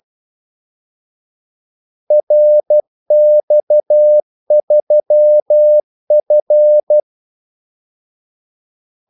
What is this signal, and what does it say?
Morse code, heard as a keyed tone, about 12 wpm, 600 Hz. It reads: RX3F
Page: 2 ekstra ordmellemrum mellem kaldesignalerne